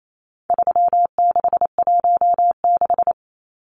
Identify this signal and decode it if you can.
Morse code, heard as a keyed tone, about 28 wpm, 700 Hz. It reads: 3616
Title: Numbers - 4-Digit 28wpm